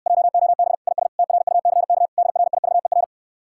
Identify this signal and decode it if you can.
Morse code, heard as a keyed tone, about 67 wpm, 700 Hz. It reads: JOB IS TRUCK DRIVER